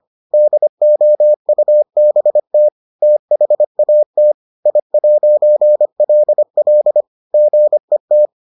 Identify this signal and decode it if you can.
Morse code, heard as a keyed tone, about 25 wpm, 600 Hz. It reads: DOUBT THAT I'LL GET